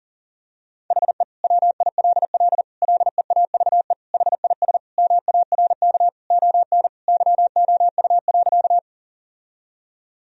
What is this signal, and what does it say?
Morse code, heard as a keyed tone, about 40 wpm, 700 Hz. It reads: HE WILL LEAVE HIS MARK ON YOU.
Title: Sentences from Top 1000 Words - Review Part 1 - 40wpm